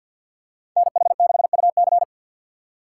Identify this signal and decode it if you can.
Morse code, heard as a keyed tone, about 47 wpm, 700 Hz. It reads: NH6UC
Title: USA Call Signs - 5 Characters - 3x Character Spacing - 50wpm